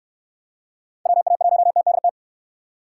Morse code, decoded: WA1TDT

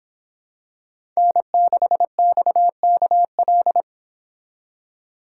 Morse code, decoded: N6XKL

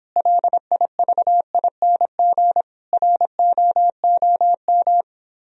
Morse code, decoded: LIVING ROOM

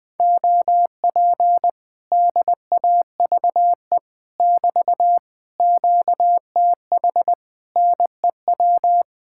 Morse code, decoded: OP DAVE <BT> QTH NEW

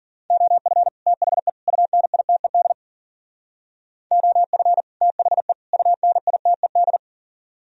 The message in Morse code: OF THE UNITED OF THE UNITED

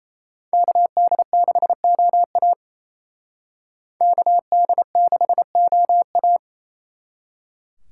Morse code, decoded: KD6OA KD6OA

